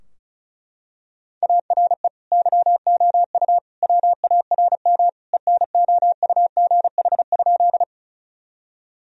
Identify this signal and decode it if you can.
Morse code, heard as a keyed tone, about 36 wpm, 700 Hz. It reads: ARE YOU WARM ENOUGH?
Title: Sentences from Top 1000 Words - Review Part 2 - 35wpm